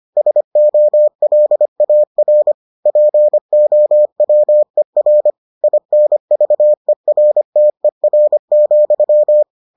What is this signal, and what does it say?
Morse code, heard as a keyed tone, about 25 wpm, 600 Hz. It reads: SOLAR POWER INVERTER,